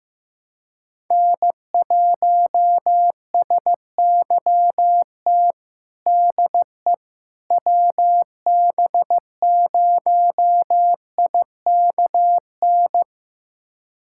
Morse code, heard as a keyed tone, about 15 wpm, 700 Hz. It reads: N1SYT DE WB0IKN
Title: USA Callsign DE Callsign 15wpm